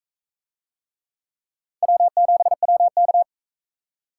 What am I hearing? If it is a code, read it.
Morse code, heard as a keyed tone, about 41 wpm, 700 Hz. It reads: W7WK